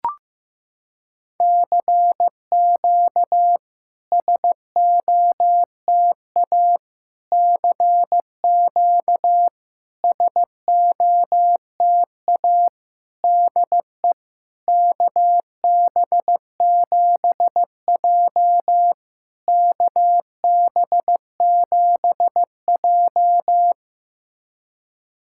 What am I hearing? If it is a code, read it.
Morse code, heard as a keyed tone, about 15 wpm, 700 Hz. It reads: CQ SOTA CQ SOTA DE KB7J KB7J
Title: SOTA QSO Practice 15wpm